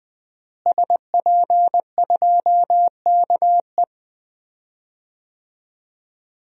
Morse code, heard as a keyed tone, about 20 wpm, 700 Hz. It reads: SP2KE
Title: DXCC-100 Call Signs - 5 Characters - Repeated 3x - 20wpm